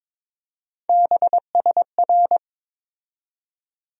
Morse code, heard as a keyed tone, about 21 wpm, 700 Hz. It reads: BSR
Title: USA Call Sign Postfixes - 3 Letters 22wpm